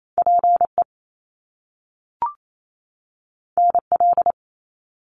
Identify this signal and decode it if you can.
Morse code, heard as a keyed tone, about 28 wpm, 700 Hz. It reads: PE NL
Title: Canadian Provinces and Territories 28wpm